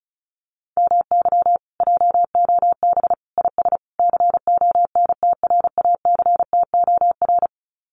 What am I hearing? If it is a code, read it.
Morse code, heard as a keyed tone, about 36 wpm, 700 Hz. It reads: MY JOB IS CONTRACTOR
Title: Common QSO Phrases Encore 35wpm